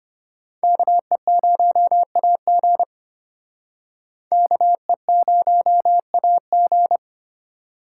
Morse code, heard as a keyed tone, about 31 wpm, 700 Hz. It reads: KE0AG KE0AG
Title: Speed Racing - USA 2x2 Call Signs 25wpm